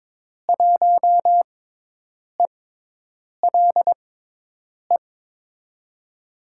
Morse code, heard as a keyed tone, about 22 wpm, spaced as for 8 wpm, 700 Hz. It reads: 1ELE
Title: USA Call Signs - 6 Characters - 6x Character Spacing - 22wpm